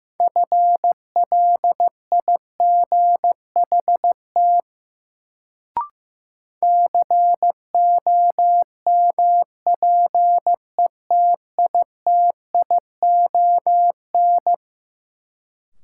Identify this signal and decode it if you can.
Morse code, heard as a keyed tone, about 15 wpm, 700 Hz. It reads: FLIGHT COMPETITION
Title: Sentences from Top 1000 Words - New Word Drill - 15wpm